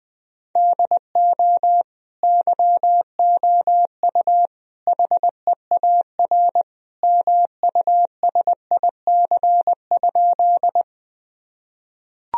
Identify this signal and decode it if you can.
Morse code, heard as a keyed tone, about 20 wpm, 700 Hz. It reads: DO YOU HEAR MUSIC?
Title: Sentences from Top 1000 Words - Review Part 1 - 20wpm